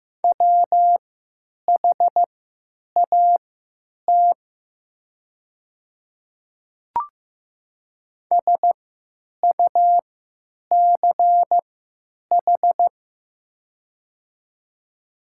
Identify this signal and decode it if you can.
Morse code, heard as a keyed tone, about 15 wpm, spaced as for 9 wpm, 700 Hz. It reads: WHAT SUCH
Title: 3x Character Spacing - Top 200 Words - 15wpm